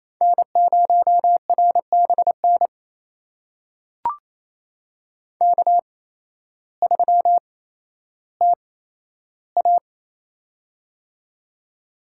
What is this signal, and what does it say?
Morse code, heard as a keyed tone, about 28 wpm, spaced as for 13 wpm, 700 Hz. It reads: N0RBN K3TA
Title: USA Call Signs - 8x Character Spacing - 28wpm